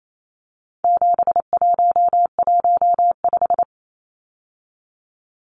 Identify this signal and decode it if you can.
Morse code, heard as a keyed tone, about 28 wpm, 700 Hz. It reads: 7115
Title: Numbers - 4-Digit 28wpm